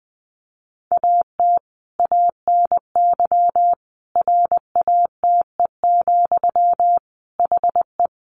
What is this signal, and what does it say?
Morse code, heard as a keyed tone, about 20 wpm, 700 Hz. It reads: AT ANY RATE, HE
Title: Sentences from Top 900 Words 20wpm